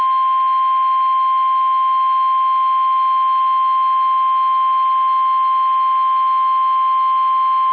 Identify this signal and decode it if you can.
Emergency Alert System (SAME) tones (the 1050 Hz attention tone). It is heard from 0.7 to 6.8 s.
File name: eas / KUSO / KUSO